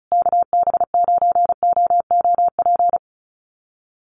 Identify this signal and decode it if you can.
Morse code, heard as a keyed tone, about 34 wpm, 700 Hz. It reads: KB9OOP
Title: USA Call Signs Repeated 3x 35wpm